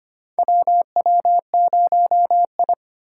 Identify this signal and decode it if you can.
Morse code, heard as a keyed tone, about 25 wpm, 690 Hz. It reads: WW0I
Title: USA Call Signs - 4 Characters - 4x Character Spacing - 25wpm